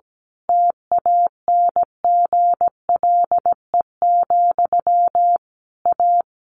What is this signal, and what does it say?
Morse code, heard as a keyed tone, about 17 wpm, 700 Hz. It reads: TANGLE, A
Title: A Sound Of Thunder 17wpm